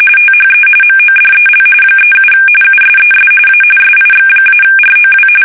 RTTY, about 45.45 baud. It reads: THE MODEL 37 USES A SIX-ROW
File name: fsk850 / teletype-corp